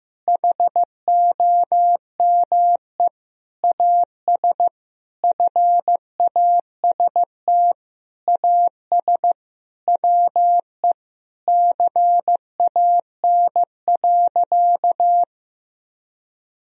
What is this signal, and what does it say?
Morse code, heard as a keyed tone, about 15 wpm, 700 Hz. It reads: HOME AS FAST AS WE CAN.
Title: Sentences from Top 1000 Words - Review Part 2 - 15wpm